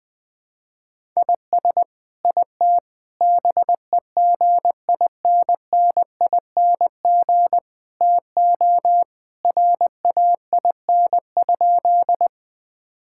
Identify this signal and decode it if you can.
Morse code, heard as a keyed tone, about 20 wpm, 700 Hz. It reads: IS IT BEGINNING TO RAIN?
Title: Sentences from Top 1000 Words - Review Part 5 - 20wpm